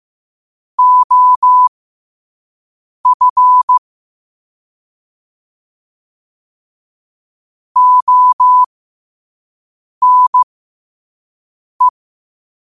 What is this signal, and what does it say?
Morse code, heard as a keyed tone, about 15 wpm, spaced as for 5 wpm, 1000 Hz. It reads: OF ONE
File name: qotdfiles / 5wpm